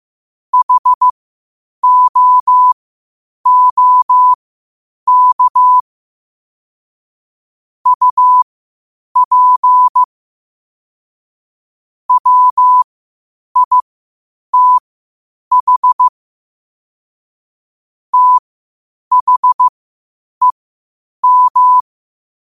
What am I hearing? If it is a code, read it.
Morse code, heard as a keyed tone, about 15 wpm, spaced as for 8 wpm, 1000 Hz. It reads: HOOK UP WITH THEM
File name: qotdfiles / 8wpm